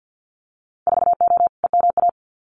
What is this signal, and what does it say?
Morse code, heard as a keyed tone, about 50 wpm, 700 Hz. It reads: 4O EMA
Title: Winter Field Day Exchanges Encore 50wpm